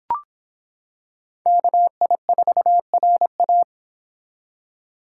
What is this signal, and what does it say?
Morse code, heard as a keyed tone, about 26 wpm, 700 Hz. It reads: KI4RA